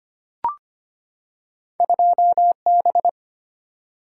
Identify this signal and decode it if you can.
Morse code, heard as a keyed tone, about 25 wpm, 700 Hz. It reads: 2B
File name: Two Letter-Number Combinations - 25wpm